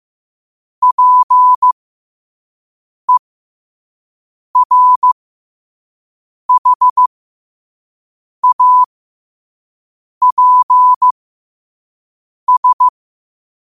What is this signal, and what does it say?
Morse code, heard as a keyed tone, about 15 wpm, spaced as for 5 wpm, 1000 Hz. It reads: PERHAPS
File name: qotdfiles / 5wpm